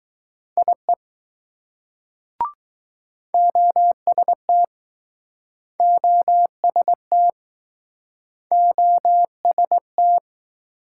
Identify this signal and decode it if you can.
Morse code, heard as a keyed tone, about 21 wpm, 700 Hz. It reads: IE OST OST OST